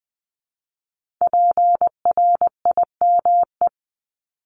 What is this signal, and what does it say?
Morse code, heard as a keyed tone, about 20 wpm, 700 Hz. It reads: PRIME
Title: Multi-letter Phonemes 20wpm